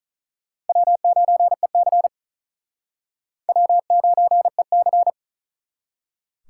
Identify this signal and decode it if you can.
Morse code, heard as a keyed tone, about 46 wpm, 700 Hz. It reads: W9EC W9EC